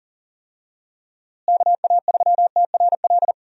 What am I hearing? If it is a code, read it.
Morse code, heard as a keyed tone, about 40 wpm, 700 Hz. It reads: KA3TRL
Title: USA Call Signs - 6 Characters - Repeated 3x - 40wpm